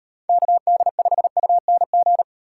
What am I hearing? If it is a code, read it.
Morse code, heard as a keyed tone, about 36 wpm, 700 Hz. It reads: KD5UNG